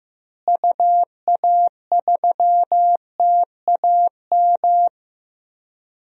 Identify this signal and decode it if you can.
Morse code, heard as a keyed tone, about 15 wpm, 700 Hz. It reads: UA3TAM
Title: DXCC-100 Call Signs - 6 Characters - Repeated 2x - 15wpm